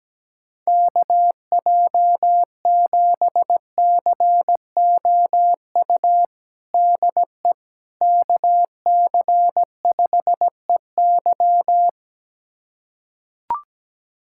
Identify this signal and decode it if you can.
Morse code, heard as a keyed tone, about 17 wpm, 700 Hz. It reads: KJ7COU DE KC5EY
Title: USA Callsign DE Callsign 17wpm